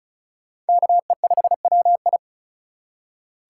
Morse code, heard as a keyed tone, about 36 wpm, 700 Hz. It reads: KE5WI
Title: USA Call Signs - 5 Characters - 2x Character Spacing - 35wpm